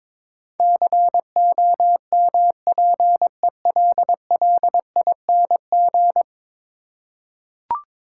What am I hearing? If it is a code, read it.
Morse code, heard as a keyed tone, about 22 wpm, 700 Hz. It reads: COMPELLING
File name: practice / 22wpm / 10 Letter Words - First 3 Letters Given - 22wpm